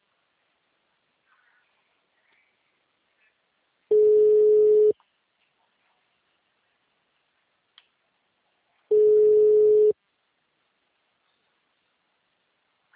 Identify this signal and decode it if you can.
Ringback tone (Europe, 425 Hz).